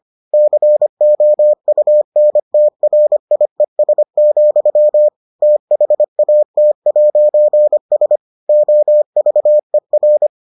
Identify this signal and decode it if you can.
Morse code, heard as a keyed tone, about 25 wpm, 600 Hz. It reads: COUNTRIES, THAT'S OVER